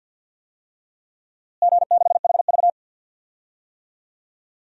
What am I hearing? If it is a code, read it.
Morse code, heard as a keyed tone, about 53 wpm, 700 Hz. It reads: G6HV
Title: DXCC-100 Call Signs - 4 Characters - Repeated 3x - 50wpm